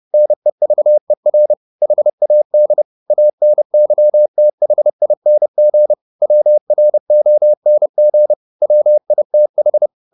Morse code, heard as a keyed tone, about 30 wpm, 600 Hz. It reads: NEVER HAD ANYTHING WRONG WITH